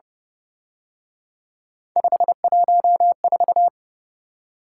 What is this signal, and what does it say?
Morse code, heard as a keyed tone, about 30 wpm, 700 Hz. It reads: 514